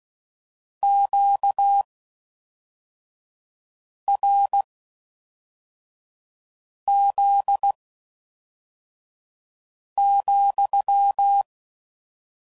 Morse code, heard as a keyed tone, about 16 wpm, spaced as for 4 wpm, 800 Hz. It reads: QRZ,